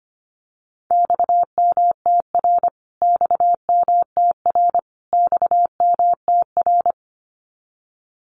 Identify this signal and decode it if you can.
Morse code, heard as a keyed tone, about 25 wpm, 700 Hz. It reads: XMTR XMTR XMTR
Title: Abbreviations Repeated 3x - 25wpm